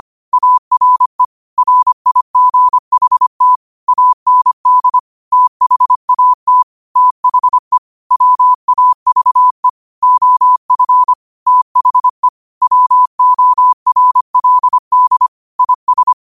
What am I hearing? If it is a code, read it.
Morse code, heard as a keyed tone, about 25 wpm, 1000 Hz. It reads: ARE RIGHT AND THAT THE WAVE OF THE WORLD IS
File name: qotdfiles / 25wpm